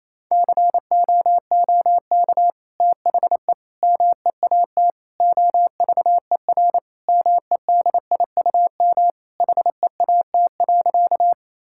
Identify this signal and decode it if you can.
Morse code, heard as a keyed tone, about 28 wpm, 700 Hz. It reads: COOK THE MEAT OVER MEDIUM HEAT.
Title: Sentences from Top 1000 Words - Review Part 6 - 28wpm